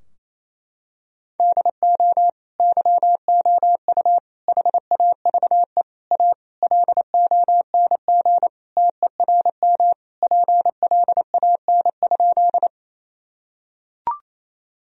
Morse code, heard as a keyed tone, about 28 wpm, 700 Hz. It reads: DO YOU HAVE A LONG TERM PLAN?
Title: Sentences from Top 1000 Words - Review Part 4 - 28wpm